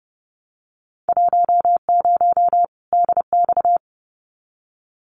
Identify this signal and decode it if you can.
Morse code, heard as a keyed tone, about 30 wpm, 700 Hz. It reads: 10 DX
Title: Winter Field Day Exchanges 30wpm